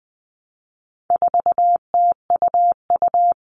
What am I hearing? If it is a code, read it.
Morse code, heard as a keyed tone, about 20 wpm, 700 Hz. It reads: 4TUU
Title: All Cut Numbers - 4-Digit 20wpm